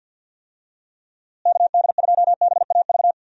Morse code, heard as a keyed tone, about 50 wpm, 700 Hz. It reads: KD2BAV
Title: USA Call Signs - 6 Characters - Repeated 3x - 50wpm